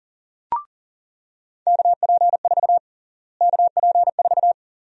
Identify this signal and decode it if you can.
Morse code, heard as a keyed tone, about 40 wpm, 700 Hz. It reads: KP4 KP4
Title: USA Call Sign Prefixes - Repeated 2x - 40wpm